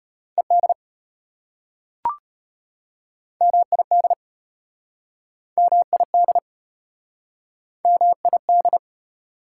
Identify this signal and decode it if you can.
Morse code, heard as a keyed tone, about 32 wpm, 700 Hz. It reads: ED MID MID MID